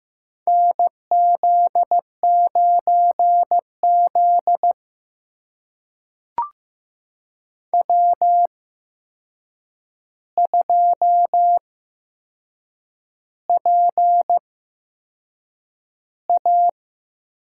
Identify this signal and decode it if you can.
Morse code, heard as a keyed tone, about 15 wpm, spaced as for 6 wpm, 700 Hz. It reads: NZ9Z W2PA
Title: USA Call Signs - 4 Characters - 8x Character Spacing - 15wpm